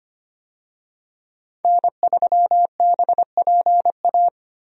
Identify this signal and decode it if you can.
Morse code, heard as a keyed tone, about 25 wpm, 700 Hz. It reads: N3BPA
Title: USA Call Signs - 5 Characters - Repeated 2x - 25wpm